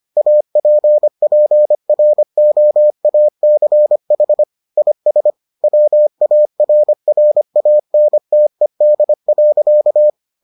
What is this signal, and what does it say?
Morse code, heard as a keyed tone, about 25 wpm, 600 Hz. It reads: APPROACH IS WARRANTED.